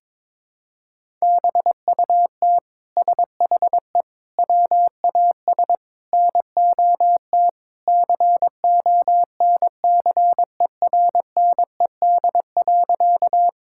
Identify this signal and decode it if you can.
Morse code, heard as a keyed tone, about 22 wpm, 700 Hz. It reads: BUT SHE WAS NOT CONCERNED.
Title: Sentences from Top 1000 Words - Review Part 6 - 22wpm